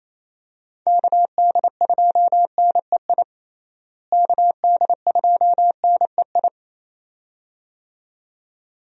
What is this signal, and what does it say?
Morse code, heard as a keyed tone, about 28 wpm, 700 Hz. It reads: KD2NEI KD2NEI
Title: USA Call Signs - 6 Characters - Repeated 2x - 28wpm